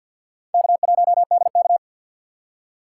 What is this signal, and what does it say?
Morse code, heard as a keyed tone, about 53 wpm, 700 Hz. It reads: K1DK
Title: USA Call Signs - 4 Characters - 6x Character Spacing - 50wpm